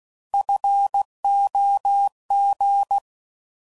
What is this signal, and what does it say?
Morse code, heard as a keyed tone, about 16 wpm, 800 Hz. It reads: FOG